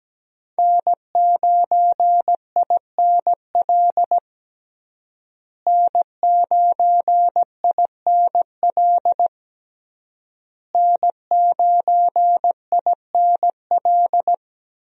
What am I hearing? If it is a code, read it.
Morse code, heard as a keyed tone, about 17 wpm, 700 Hz. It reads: N9INL N9INL N9INL
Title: USA Call Signs - 5 Characters - Repeated 3x - 17wpm